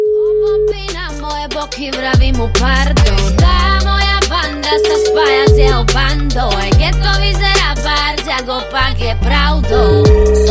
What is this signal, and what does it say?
Ringback tone (Europe, 425 Hz), with noise.